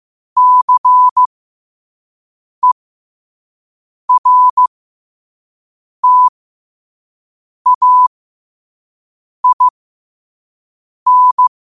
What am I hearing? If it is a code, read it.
Morse code, heard as a keyed tone, about 15 wpm, spaced as for 5 wpm, 1000 Hz. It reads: CERTAIN
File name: qotdfiles / 5wpm